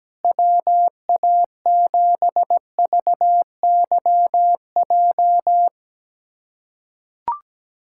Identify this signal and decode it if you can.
Morse code, heard as a keyed tone, about 17 wpm, 700 Hz. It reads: WA7VYJ
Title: USA Callsign DE Callsign 17wpm